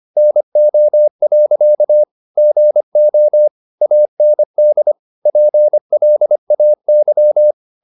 Morse code, heard as a keyed tone, about 25 wpm, 600 Hz. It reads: NO. GO AND PLAY
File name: the old man and the sea part 1 25wpm